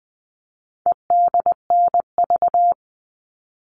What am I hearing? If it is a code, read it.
Morse code, heard as a keyed tone, about 20 wpm, 700 Hz. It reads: EDNV
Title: All Cut Numbers - 4-Digit 20wpm